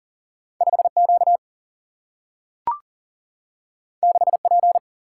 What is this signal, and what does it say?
Morse code, heard as a keyed tone, about 39 wpm, 700 Hz. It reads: 5Q 6P
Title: Single-Letter Single-Number 40wpm